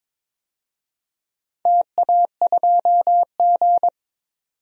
Morse code, heard as a keyed tone, about 21 wpm, 700 Hz. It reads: TA2G
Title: DXCC-100 Call Signs - 4 Characters - 22wpm